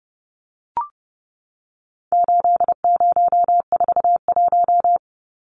Morse code, heard as a keyed tone, about 30 wpm, 700 Hz. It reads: 8041
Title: Numbers - 4-Digit 30wpm